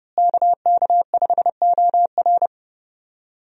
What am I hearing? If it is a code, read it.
Morse code, heard as a keyed tone, about 32 wpm, 700 Hz. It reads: KK5OR
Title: USA Call Signs - 5 Characters - Repeated 3x - 30wpm